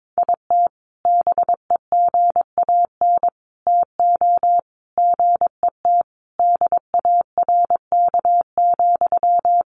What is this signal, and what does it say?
Morse code, heard as a keyed tone, about 22 wpm, 700 Hz. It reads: IT BEGAN TO GET DARK,
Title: Sentences from Top 1400 Words 22wpm